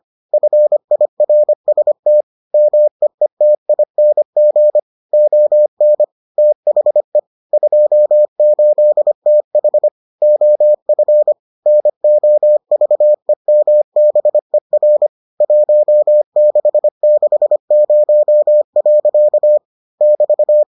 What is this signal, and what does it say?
Morse code, heard as a keyed tone, about 25 wpm, 600 Hz. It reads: FIRST MEETING ON THE 28TH OF NOVEMBER 1660. <BT>